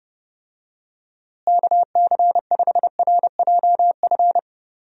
Morse code, heard as a keyed tone, about 32 wpm, 700 Hz. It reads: KC5RJF
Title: USA Call Signs - 6 Characters - Repeated 2x - 30wpm